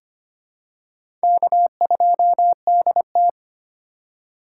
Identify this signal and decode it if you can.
Morse code, heard as a keyed tone, about 25 wpm, 700 Hz. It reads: K2DT
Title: USA Call Signs - 4 Characters - 4x Character Spacing - 25wpm